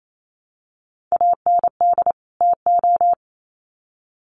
Morse code, heard as a keyed tone, about 28 wpm, 700 Hz. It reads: AND TO
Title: Sets of 2 Words 28wpm